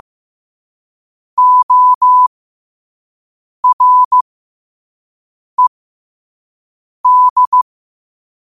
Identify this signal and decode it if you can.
Morse code, heard as a keyed tone, about 15 wpm, spaced as for 5 wpm, 1000 Hz. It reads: ORED
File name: qotdfiles / 5wpm